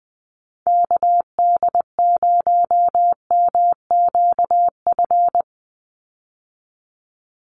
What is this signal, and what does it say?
Morse code, heard as a keyed tone, about 20 wpm, 700 Hz. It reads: KD0MQF
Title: USA Call Signs 20wpm